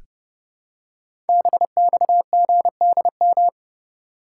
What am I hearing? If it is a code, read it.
Morse code, heard as a keyed tone, about 28 wpm, 700 Hz. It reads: BXGDM
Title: Code Groups - 5 Letters - 30wpm